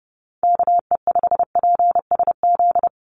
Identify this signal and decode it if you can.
Morse code, heard as a keyed tone, about 32 wpm, 700 Hz. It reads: KE5PSZ
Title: USA Call Signs 30wpm